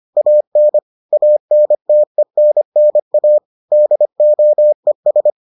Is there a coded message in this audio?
Yes: Morse code from 0.2 to 5.3 s, heard as a keyed tone, about 25 wpm, 600 Hz. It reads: AN ANTENNA DOES